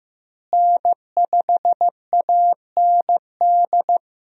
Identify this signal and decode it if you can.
Morse code, heard as a keyed tone, about 15 wpm, 700 Hz. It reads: N5AND
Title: USA Call Signs - 5 Characters - Repeated 2x - 15wpm